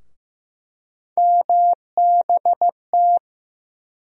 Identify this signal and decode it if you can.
Morse code, heard as a keyed tone, about 15 wpm, 700 Hz. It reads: MBT
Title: USA Call Sign Postfixes - 3 Letters 15wpm